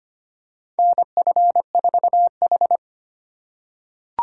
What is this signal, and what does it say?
Morse code, heard as a keyed tone, about 25 wpm, 700 Hz. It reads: NF4H
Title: USA Call Signs - 4x Character Spacing - 25wpm